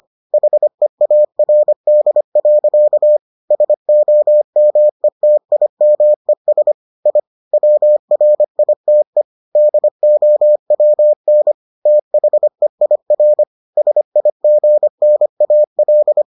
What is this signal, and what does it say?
Morse code, heard as a keyed tone, about 25 wpm, 600 Hz. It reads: HEARD. SOMETIMES I WRITE DOWN THEIR SIGNAL